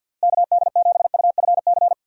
Morse code, heard as a keyed tone, about 50 wpm, 700 Hz. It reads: KD7UFC
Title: USA Call Signs - 6 Characters - Repeated 3x - 50wpm